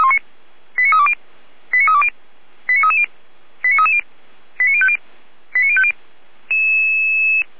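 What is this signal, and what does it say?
Five-tone selective call (ZVEI), heard as quick runs of five tones, tones of about 70 ms. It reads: 89320 89320 89330 89330 89050 89050
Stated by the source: ZVEI five-tone call 89305 received by SDR, from the feuerwehr-orlamuende.org (fire brigade Orlamünde) audio archive.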